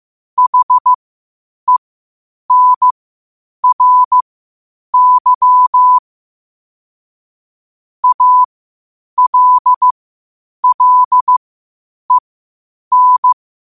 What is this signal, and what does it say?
Morse code, heard as a keyed tone, about 15 wpm, spaced as for 8 wpm, 1000 Hz. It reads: HENRY ALLEN
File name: qotdfiles / 8wpm